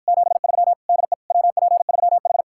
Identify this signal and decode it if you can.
Morse code, heard as a keyed tone, about 54 wpm, 700 Hz. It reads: 73 DE WP3H